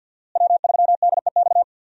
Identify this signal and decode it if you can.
Morse code, heard as a keyed tone, about 52 wpm, 700 Hz. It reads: W3DEX
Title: USA Call Signs - 5 Characters - Repeated 2x - 50wpm